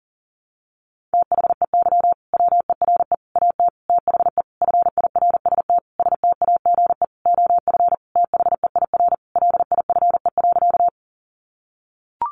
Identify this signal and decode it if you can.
Morse code, heard as a keyed tone, about 40 wpm, 700 Hz. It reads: THEY WERE AT THE FIRST STAGE OF THEIR LIFE.
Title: Sentences from Top 1200 Words 40wpm